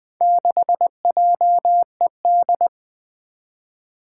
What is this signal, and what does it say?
Morse code, heard as a keyed tone, about 20 wpm, 700 Hz. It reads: 6JED
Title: Four Challenging Letter-Number Combinations 20wpm